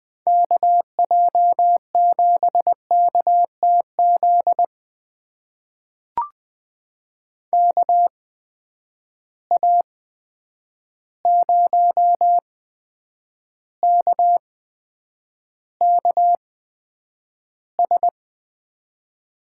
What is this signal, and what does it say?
Morse code, heard as a keyed tone, about 20 wpm, spaced as for 9 wpm, 700 Hz. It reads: KJ7KTZ KA0KKS
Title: USA Call Signs - 8x Character Spacing - 20wpm